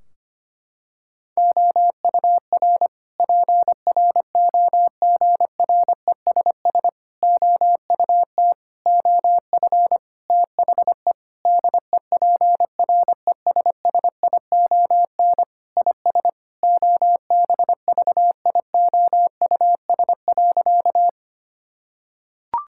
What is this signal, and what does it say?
Morse code, heard as a keyed tone, about 25 wpm, 700 Hz. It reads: OUR PROGRESS OUT OF THE DEPRESSION IS OBVIOUS.
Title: Sentences from Top 1600 Words 25wpm